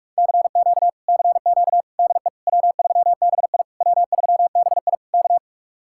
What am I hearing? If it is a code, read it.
Morse code, heard as a keyed tone, about 49 wpm, 700 Hz. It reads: CQ CQ DE W3BI W3BI K